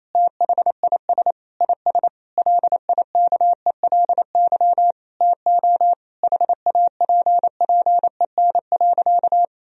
Morse code, heard as a keyed tone, about 28 wpm, 700 Hz. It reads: THIS IS LIKELY TO HAPPEN.
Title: Sentences from Top 1000 Words - Review Part 4 - 28wpm